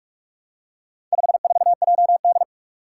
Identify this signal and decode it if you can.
Morse code, heard as a keyed tone, about 44 wpm, 700 Hz. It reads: 54JD
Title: Four Challenging Letter-Number Combinations 45wpm